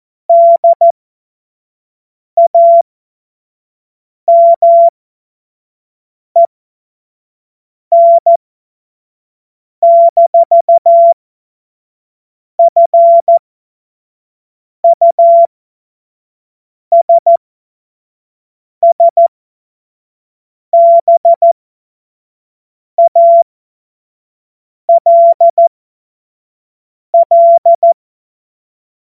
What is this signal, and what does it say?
Morse code, heard as a keyed tone, about 14 wpm, spaced as for 5 wpm, 680 Hz. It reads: DAMEN-FUSSBALL
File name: Morsekurs / Witze / Witz 27. April 2021, 25 F70